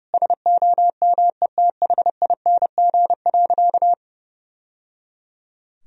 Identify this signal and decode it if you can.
Morse code, heard as a keyed tone, about 30 wpm, 700 Hz. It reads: SOMETHING.
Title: Sentences from Top 1000 Words - Review Part 3 - 30wpm